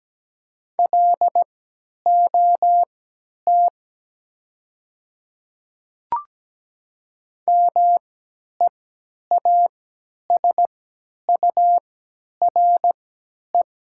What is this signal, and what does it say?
Morse code, heard as a keyed tone, about 17 wpm, spaced as for 10 wpm, 700 Hz. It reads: LOT MEASURE
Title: 3x Character Spacing - Top 500 Words - 17wpm